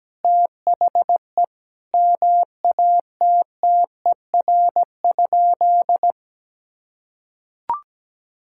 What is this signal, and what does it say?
Morse code, heard as a keyed tone, about 17 wpm, 700 Hz. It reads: THE MATTER?
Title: Sentences from Top 1000 Words - Review Part 6 - 17wpm